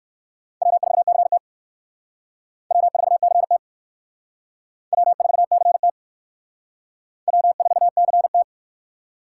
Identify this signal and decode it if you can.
Morse code, heard as a keyed tone, about 61 wpm, 700 Hz. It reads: W4CT W4CT W4CT W4CT